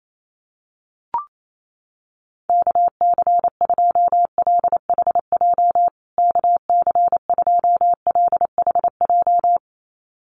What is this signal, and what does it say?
Morse code, heard as a keyed tone, about 28 wpm, 700 Hz. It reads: KC2LHJ KC2LHJ
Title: USA Call Signs Repeated 2x 28wpm